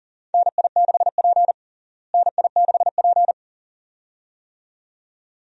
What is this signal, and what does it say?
Morse code, heard as a keyed tone, about 40 wpm, 700 Hz. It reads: NI6P NI6P
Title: USA Call Signs - 4 Characters - Repeated 2x - 40wpm